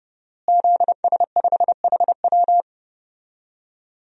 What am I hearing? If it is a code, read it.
Morse code, heard as a keyed tone, about 32 wpm, 700 Hz. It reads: ZS5HW
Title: DXCC-100 Call Signs - 5 Characters - Repeated 3x - 30wpm